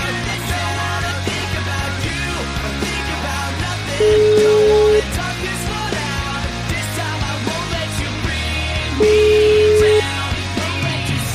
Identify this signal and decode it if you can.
Ringback tone (Europe, 425 Hz), with noise.